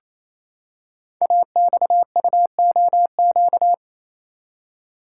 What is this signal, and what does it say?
Morse code, heard as a keyed tone, about 28 wpm, 700 Hz. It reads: AXUOQ
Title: Code Groups - 5 Letters - 28wpm